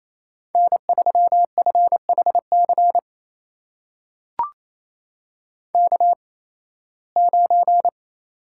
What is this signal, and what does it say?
Morse code, heard as a keyed tone, about 28 wpm, spaced as for 18 wpm, 700 Hz. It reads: N3FHC K9
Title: USA Call Signs - 5 Characters - 8x Character Spacing - 28wpm